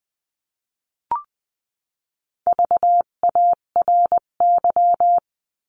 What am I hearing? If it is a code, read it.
Morse code, heard as a keyed tone, about 20 wpm, 700 Hz. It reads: VARY
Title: 258 Lesson 42c - Words 16-20wpm